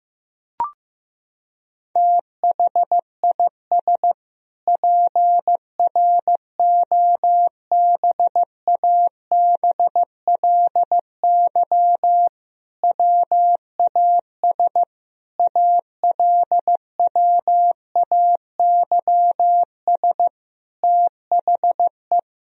Morse code, heard as a keyed tone, about 15 wpm, 700 Hz. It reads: THIS PROBABLY WAS ALWAYS THE
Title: Sentences from Top 1000 Words - Review Part 4 - 15wpm